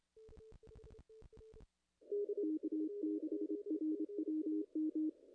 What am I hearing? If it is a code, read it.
Morse code, heard as a keyed tone, about 26 wpm, 310 Hz, with radio noise. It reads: K6RWM
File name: contests / ss-cw-k5tr-op-n5rz -0100